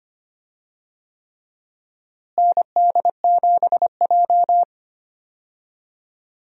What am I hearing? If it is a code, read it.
Morse code, heard as a keyed tone, about 25 wpm, 700 Hz. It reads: ND7J
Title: USA Call Signs - 4 Characters - Repeated 3x - 25wpm